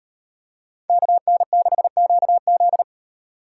Morse code, heard as a keyed tone, about 36 wpm, 700 Hz. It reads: KN6QZ